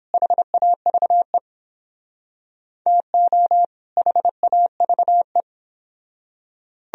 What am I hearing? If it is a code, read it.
Morse code, heard as a keyed tone, about 30 wpm, 700 Hz. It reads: HAVE TO HAVE